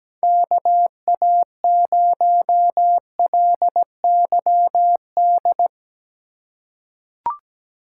Morse code, heard as a keyed tone, about 17 wpm, 700 Hz. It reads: KA0LYD
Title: USA Call Signs - 2x Character Spacing - 17wpm